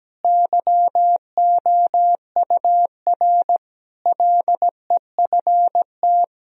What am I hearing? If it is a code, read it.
Morse code, heard as a keyed tone, about 17 wpm, 700 Hz. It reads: YOUR LEFT
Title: Sentences from Top 1000 Words - Review Part 2 - 17wpm